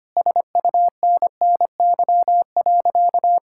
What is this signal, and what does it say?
Morse code, heard as a keyed tone, about 25 wpm, 700 Hz. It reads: SUNNY.
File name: practice / 25wpm / Sentences with 2-Word Phrases - 25wpm